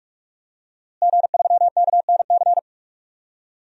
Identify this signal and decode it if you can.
Morse code, heard as a keyed tone, about 46 wpm, 700 Hz. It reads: G3KNC
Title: DXCC-100 Call Signs - 5 Characters - Repeated 3x - 45wpm